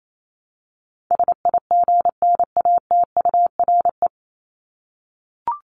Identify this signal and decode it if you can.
Morse code, heard as a keyed tone, about 28 wpm, 700 Hz. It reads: SIGNATURE
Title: Top 1751-2000 Words 28wpm